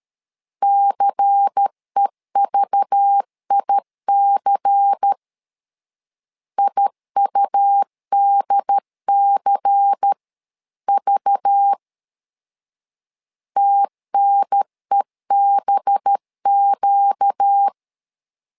Morse code, heard as a keyed tone, about 13 wpm, 800 Hz. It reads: CEVIC IUDCV TNEBQ
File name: LEZIONI / 5 LBVC / 5 LBVC 12wpm 1955